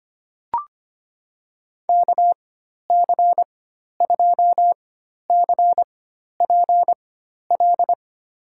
Morse code, heard as a keyed tone, about 25 wpm, spaced as for 12 wpm, 700 Hz. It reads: KC2CPL
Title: USA Call Signs - 6 Characters - 4x Character Spacing - 25wpm